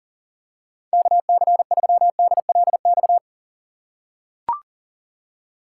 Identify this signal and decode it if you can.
Morse code, heard as a keyed tone, about 40 wpm, 700 Hz. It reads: KC3DLX
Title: USA Call Signs - 6 Characters - 4x Character Spacing - 40wpm